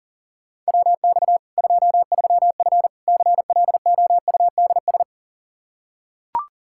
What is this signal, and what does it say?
Morse code, heard as a keyed tone, about 40 wpm, 700 Hz. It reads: WX 23F CLOUDS
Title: Practice QSO - Exchange 2 - Info - 40wpm